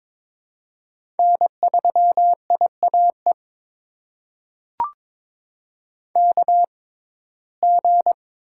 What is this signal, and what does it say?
Morse code, heard as a keyed tone, about 22 wpm, spaced as for 16 wpm, 700 Hz. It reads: N3IAE KG
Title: USA Call Signs - 5 Characters - 6x Character Spacing - 22wpm